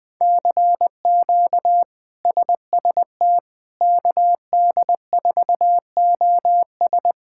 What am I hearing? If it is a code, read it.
Morse code, heard as a keyed tone, about 21 wpm, 700 Hz. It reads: CQ SST KD4OS
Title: K1USN SST QSO Practice - 20wpm